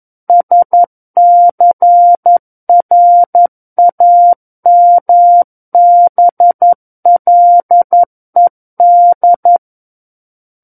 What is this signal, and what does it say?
Morse code, heard as a keyed tone, about 11 wpm, 700 Hz. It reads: SCRAMBLED
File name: Book One / 02-Chapter 2 - 11 WPM